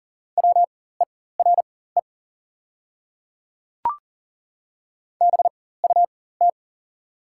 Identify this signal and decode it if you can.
Morse code, heard as a keyed tone, about 40 wpm, spaced as for 19 wpm, 690 Hz. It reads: WERE BUT
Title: 4x Character Spacing - Top 100 Words - 40wpm